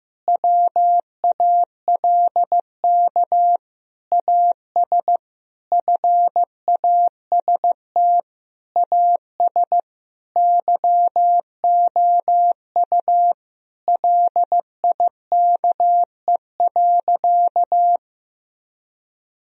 Morse code, heard as a keyed tone, about 15 wpm, 700 Hz. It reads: WALK AS FAST AS YOU LIKE.
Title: Sentences from Top 1000 Words - Review Part 2 - 15wpm